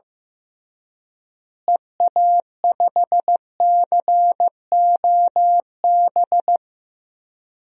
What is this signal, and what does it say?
Morse code, heard as a keyed tone, about 16 wpm, 700 Hz. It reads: EA5COB
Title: DXCC-100 Call Signs - 6 Characters - Repeated 3x - 15wpm